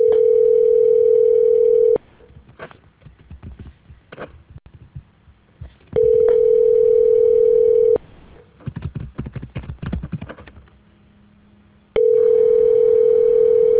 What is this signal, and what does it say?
Ringback tone (North America), 440 + 480 Hz.